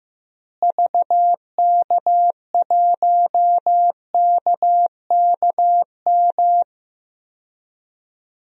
Morse code, heard as a keyed tone, about 15 wpm, 700 Hz. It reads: VK1KKM
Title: DXCC-100 Call Signs - 6 Characters - Repeated 3x - 15wpm